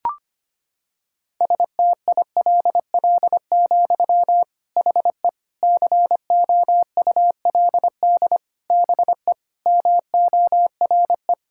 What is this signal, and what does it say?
Morse code, heard as a keyed tone, about 25 wpm, 700 Hz. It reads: STILL, HE COULD BE MORE
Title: Sentences from Top 1000 Words - Review Part 5 - 25wpm